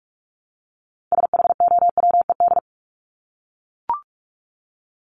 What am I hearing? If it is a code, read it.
Morse code, heard as a keyed tone, about 44 wpm, 700 Hz. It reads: SHOWED
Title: ED Suffix - Top 100 Words with Base Words 45wpm